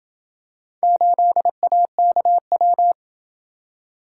Morse code, heard as a keyed tone, about 27 wpm, 700 Hz. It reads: 8AKW